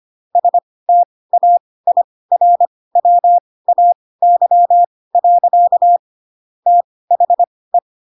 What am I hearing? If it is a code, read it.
Morse code, heard as a keyed tone, about 25 wpm, spaced as for 18 wpm, 700 Hz. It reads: STAIRWAY. THE